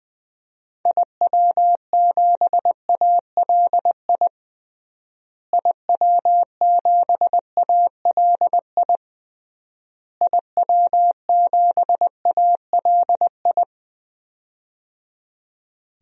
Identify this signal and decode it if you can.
Morse code, heard as a keyed tone, about 20 wpm, 700 Hz. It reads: IW7ALI IW7ALI IW7ALI
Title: DXCC-100 Call Signs - 6 Characters - Repeated 3x - 20wpm